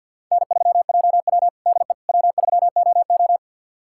Morse code, heard as a keyed tone, about 52 wpm, 700 Hz. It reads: N3JW DE W3OO
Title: USA Callsign DE Callsign - 4 Characters - 50wpm